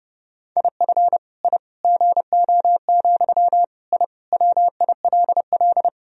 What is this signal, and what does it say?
Morse code, heard as a keyed tone, about 32 wpm, 700 Hz. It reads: IF I GO, I WILL
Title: Sentences from Top 1000 Words - Review Part 3 - 30wpm